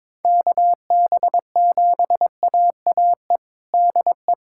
Morse code, heard as a keyed tone, about 22 wpm, 700 Hz. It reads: KB7AAE DE
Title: USA Callsign DE Callsign - 6 Characters - 22wpm